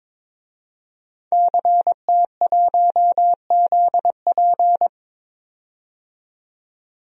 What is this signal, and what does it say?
Morse code, heard as a keyed tone, about 22 wpm, 700 Hz. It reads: CT1ZP
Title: DXCC-100 Call Signs - 5 Characters - 22wpm